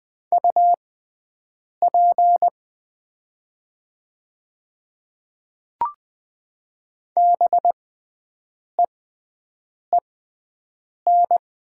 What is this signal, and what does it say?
Morse code, heard as a keyed tone, about 20 wpm, spaced as for 7 wpm, 690 Hz. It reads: UP BEEN
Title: 6x Character Spacing - Top 200 Words - 20wpm